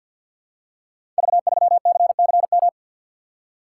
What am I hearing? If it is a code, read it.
Morse code, heard as a keyed tone, about 48 wpm, 700 Hz. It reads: V3CCM
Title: Code Groups - 5 Letter:Numbers - 50wpm